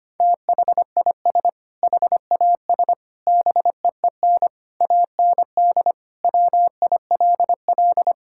Morse code, heard as a keyed tone, about 25 wpm, 700 Hz. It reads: THIS HAS BEEN AND WILL
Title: Sentences from Top 1600 Words 25wpm